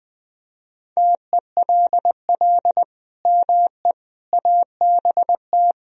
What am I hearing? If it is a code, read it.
Morse code, heard as a keyed tone, about 21 wpm, 700 Hz. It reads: TELL ME ABT
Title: Sentences from Top 200 Words and Abbreviations 20wpm